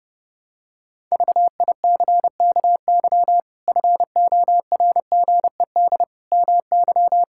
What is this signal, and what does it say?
Morse code, heard as a keyed tone, about 30 wpm, 700 Hz. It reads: VICKY FORGED MY
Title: Pangram Challenge v2 - 30wpm